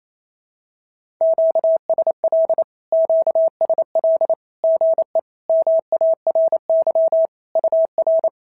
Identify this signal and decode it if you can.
Morse code, heard as a keyed tone, about 28 wpm, 650 Hz. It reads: QSL QSL GE MARY UR
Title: SOTA QSO Practice 28wpm